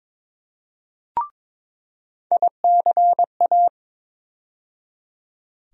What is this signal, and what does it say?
Morse code, heard as a keyed tone, about 22 wpm, 700 Hz. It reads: ICA
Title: USA Call Sign Postfixes - 3 Letters 22wpm